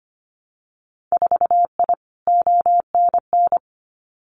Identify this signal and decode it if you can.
Morse code, heard as a keyed tone, about 25 wpm, 700 Hz. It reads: 4I ONN
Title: Winter Field Day Exchanges Encore 25wpm